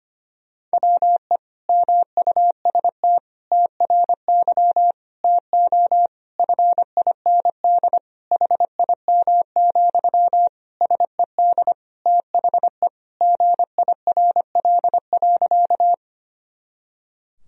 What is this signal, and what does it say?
Morse code, heard as a keyed tone, about 25 wpm, 700 Hz. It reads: WE MUST TRY TO FIND HIM, SED THE GIRL.
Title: Sentences from Top 400 Words and Abbreviations 25wpm